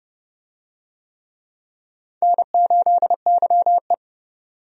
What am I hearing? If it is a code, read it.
Morse code, heard as a keyed tone, about 30 wpm, 700 Hz. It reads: N8YE
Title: USA Call Signs - 4 Characters - 30wpm